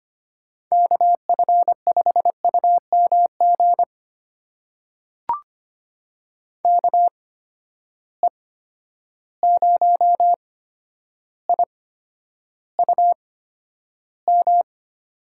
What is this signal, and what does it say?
Morse code, heard as a keyed tone, about 25 wpm, spaced as for 11 wpm, 700 Hz. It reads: KF5UMG KE0IUM
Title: USA Call Signs - 6 Characters - 8x Character Spacing - 25wpm